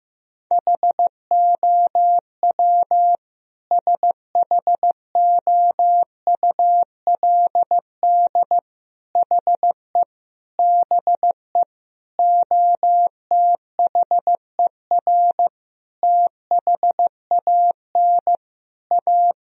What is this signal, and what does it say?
Morse code, heard as a keyed tone, about 15 wpm, 700 Hz. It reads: HOW SHOULD HE BE OTHER THAN A
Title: Sentences from Top 1000 Words - Review Part 2 - 15wpm